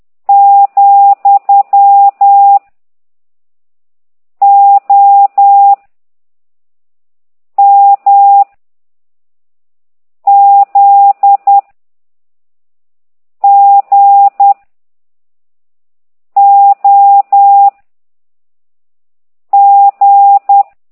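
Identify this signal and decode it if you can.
Morse code, heard as a keyed tone, about 10 wpm, 800 Hz. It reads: , O M Z G O G
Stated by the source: Morse code lesson 5 exercise 10 by CT2IDW, containing only characters M, O, G, Z, Q and comma.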